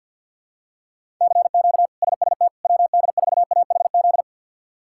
Morse code, heard as a keyed tone, about 51 wpm, 700 Hz. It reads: CQ SST WD4AHZ
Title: K1USN SST QSO Practice - 50wpm